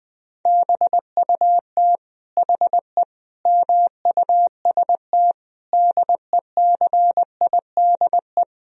Morse code, heard as a keyed tone, about 21 wpm, 700 Hz. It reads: BUT HE MUST DECIDE
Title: Sentences from Top 1000 Words - Review Part 3 - 20wpm